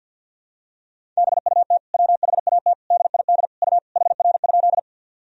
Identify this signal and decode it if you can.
Morse code, heard as a keyed tone, about 51 wpm, 700 Hz. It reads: BUT WHAT DID U HR?
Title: Sentences from Top 300 Words and Abbreviations 50wpm